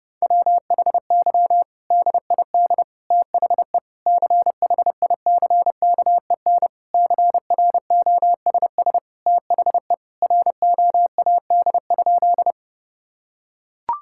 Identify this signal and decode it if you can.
Morse code, heard as a keyed tone, about 30 wpm, 700 Hz. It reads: WHY DID THE CHICKEN CROSS THE ROAD?
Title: Sentences from Top 1000 Words - Review Part 5 - 30wpm